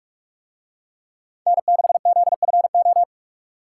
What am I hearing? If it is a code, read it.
Morse code, heard as a keyed tone, about 41 wpm, 700 Hz. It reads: N6ZFO